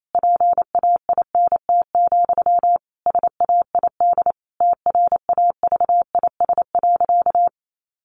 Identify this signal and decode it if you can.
Morse code, heard as a keyed tone, about 28 wpm, 700 Hz. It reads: PAINT, SAID TRAVIS.
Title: A Sound Of Thunder 28wpm